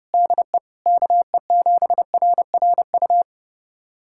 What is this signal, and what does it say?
Morse code, heard as a keyed tone, about 30 wpm, 700 Hz. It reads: DE KE7RRU
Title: USA Callsign DE Callsign 30wpm